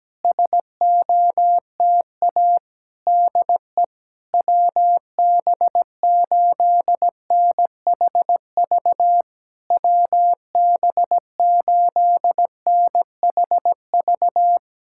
Morse code, heard as a keyed tone, about 17 wpm, 700 Hz. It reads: SOTA DE WB8NHV WB8NHV
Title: SOTA QSO Practice 17wpm